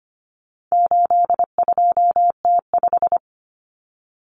Morse code, heard as a keyed tone, about 25 wpm, 700 Hz. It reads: 82T5
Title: Common Cut Numbers - 4-Digit 25wpm